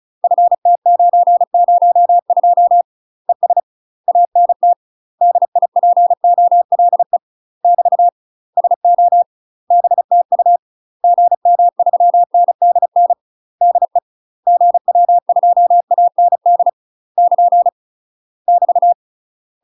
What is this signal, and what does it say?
Morse code, heard as a keyed tone, about 34 wpm, 700 Hz. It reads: FT902 ES ANT DIPOLE <BT> SO BTU GM3NDN DE GW2AND <KN> <BT>
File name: UK MASTER QSOs TRAINING / 35WPM / UK MASTER QSOs 0001